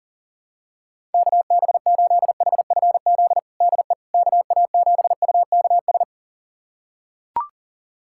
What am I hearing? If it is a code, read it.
Morse code, heard as a keyed tone, about 40 wpm, 700 Hz. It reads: KB8HFZ DE KA7UKS
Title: USA Callsign DE Callsign - 6 Characters - 40wpm